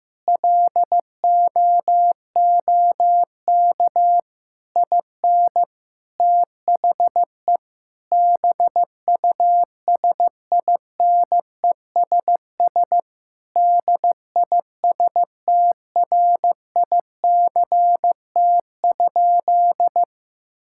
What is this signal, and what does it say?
Morse code, heard as a keyed tone, about 15 wpm, 700 Hz. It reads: LOOK IN THE BUSINESS DISTRICT?
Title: Sentences from Top 1600 Words 15wpm